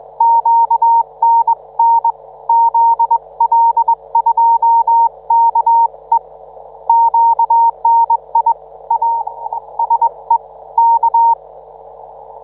Morse code, heard as a keyed tone, about 21 wpm, 910 Hz. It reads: QNN ZL2KE QNI A SE K